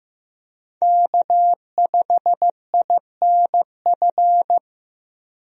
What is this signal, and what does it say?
Morse code, heard as a keyed tone, about 15 wpm, 700 Hz. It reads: K5INF
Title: USA Call Signs - 5 Characters - Repeated 3x - 15wpm